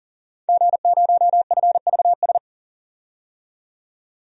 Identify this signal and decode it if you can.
Morse code, heard as a keyed tone, about 40 wpm, 700 Hz. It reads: G0FVS
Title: DXCC-100 Call Signs - 5 Characters - 40wpm